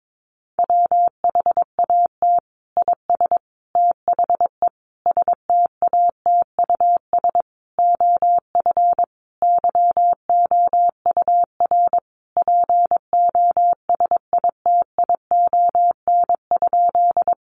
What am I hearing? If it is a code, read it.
Morse code, heard as a keyed tone, about 22 wpm, 700 Hz. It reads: WHAT IS THE STATUS OF YOUR POSITION?
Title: Sentences from Top 1400 Words 22wpm